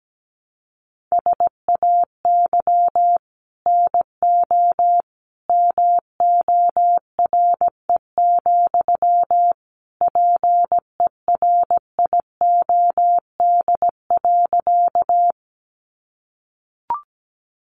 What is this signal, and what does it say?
Morse code, heard as a keyed tone, about 17 wpm, 700 Hz. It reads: SAY NO MORE, PERIOD.
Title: Sentences from Top 800 Words 17wpm